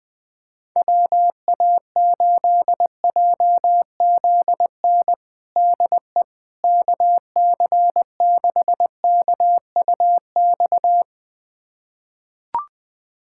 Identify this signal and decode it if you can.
Morse code, heard as a keyed tone, about 20 wpm, 700 Hz. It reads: WA8JZN DE KC6KUX
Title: USA Callsign DE Callsign - 6 Characters - 20wpm